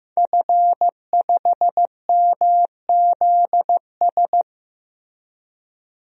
Morse code, heard as a keyed tone, about 15 wpm, 700 Hz. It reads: F5MZS
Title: DXCC-100 Call Signs - 5 Characters - Repeated 3x - 15wpm